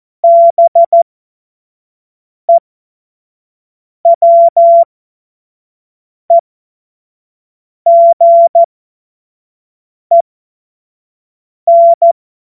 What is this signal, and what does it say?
Morse code, heard as a keyed tone, about 14 wpm, spaced as for 5 wpm, 680 Hz. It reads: BEWEGEN